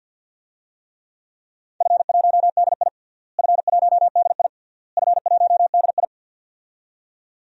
Morse code, heard as a keyed tone, about 51 wpm, 700 Hz. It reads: F1DI F1DI F1DI
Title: DXCC-100 Call Signs - 4 Characters - Repeated 3x - 50wpm